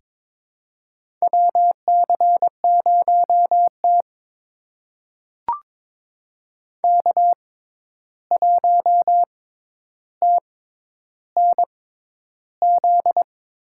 Morse code, heard as a keyed tone, about 22 wpm, spaced as for 11 wpm, 700 Hz. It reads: WC0T K1TNZ
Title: USA Call Signs - 6x Character Spacing - 22wpm